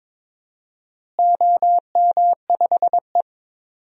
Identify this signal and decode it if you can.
Morse code, heard as a keyed tone, about 22 wpm, 700 Hz. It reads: OM5E
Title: DXCC-100 Call Signs - 4 Characters - Repeated 3x - 22wpm